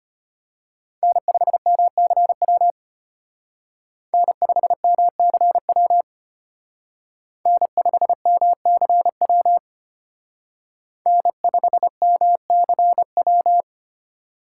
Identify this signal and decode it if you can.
Morse code, heard as a keyed tone, about 29 wpm, 700 Hz. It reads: N5MCW N5MCW N5MCW N5MCW
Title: Speed Racing - USA Call Signs 25wpm